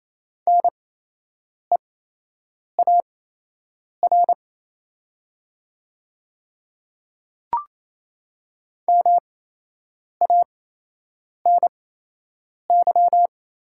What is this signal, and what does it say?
Morse code, heard as a keyed tone, about 28 wpm, spaced as for 8 wpm, 700 Hz. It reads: NEAR MANY
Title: 8x Character Spacing - Top 200 Words - 28wpm